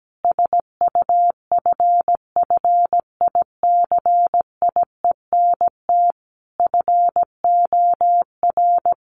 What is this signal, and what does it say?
Morse code, heard as a keyed tone, about 17 wpm, 700 Hz. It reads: SUFFICIENT FOR
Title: Sentences from Top 1400 Words 17wpm